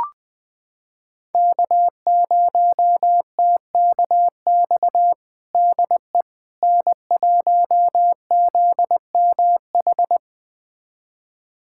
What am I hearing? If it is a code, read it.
Morse code, heard as a keyed tone, about 21 wpm, 700 Hz. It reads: K0TKX DE N1ZMH
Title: USA Callsign DE Callsign - 5 Characters - 20wpm